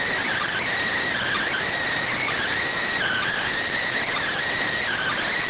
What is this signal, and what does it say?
Emergency Alert System (SAME) tones, heard as harsh buzzing bursts. It is heard from 1.2 to 5.3 s.